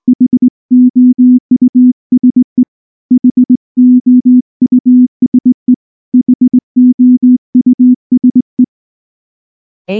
Morse code, heard as a keyed tone, about 19 wpm, 250 Hz. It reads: HOUSE HOUSE HOUSE